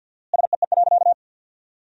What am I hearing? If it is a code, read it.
Morse code, heard as a keyed tone, about 48 wpm, 700 Hz. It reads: SEE.
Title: 4x Word-Spacing - Sentences from Top 100 Words - 50wpm